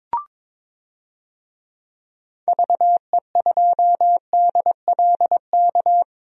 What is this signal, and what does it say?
Morse code, heard as a keyed tone, about 22 wpm, 700 Hz. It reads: VE2DLK
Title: DXCC-100 Call Signs - 6 Characters - 22wpm